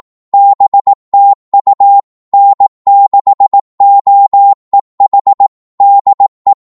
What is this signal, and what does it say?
Morse code, heard as a keyed tone, about 18 wpm, 800 Hz. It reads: BTU N6OEH DE
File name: Morse / qso-18wpm